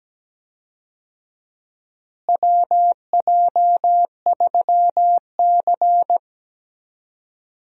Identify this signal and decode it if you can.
Morse code, heard as a keyed tone, about 17 wpm, 700 Hz. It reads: WJ3C
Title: USA Call Signs - 4 Characters - Repeated 2x - 17wpm